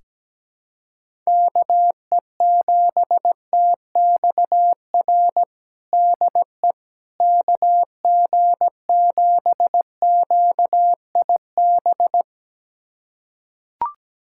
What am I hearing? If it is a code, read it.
Morse code, heard as a keyed tone, about 17 wpm, 700 Hz. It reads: KE7TXR DE KG7QIB
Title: USA Callsign DE Callsign - 6 Characters - 17wpm